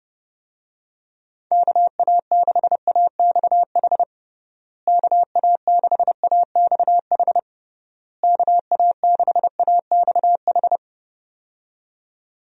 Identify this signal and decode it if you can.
Morse code, heard as a keyed tone, about 30 wpm, 700 Hz. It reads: KA6AXH KA6AXH KA6AXH
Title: USA Call Signs - 6 Characters - Repeated 3x - 30wpm